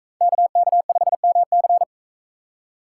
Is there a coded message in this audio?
Yes: Morse code from 0.2 to 1.8 s, heard as a keyed tone, about 42 wpm, 700 Hz. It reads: KK5MC